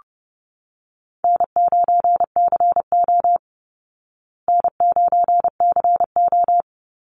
Morse code, heard as a keyed tone, about 30 wpm, 700 Hz. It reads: N9CO N9CO
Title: Easy Call Signs Repeated 2x - 30wpm